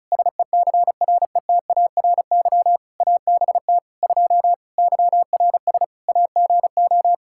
Morse code, heard as a keyed tone, about 36 wpm, 700 Hz. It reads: SECRETARY ABT 2 YRS AGO
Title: Practice QSO - Exchange 3 - More Info - 35wpm